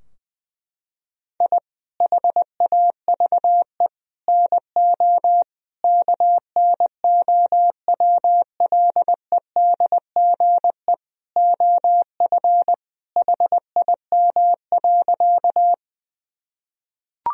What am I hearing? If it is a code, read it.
Morse code, heard as a keyed tone, about 20 wpm, 700 Hz. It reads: I HAVE NO KNOWLEDGE OF HIM.
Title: Sentences from Top 1000 Words - Review Part 3 - 20wpm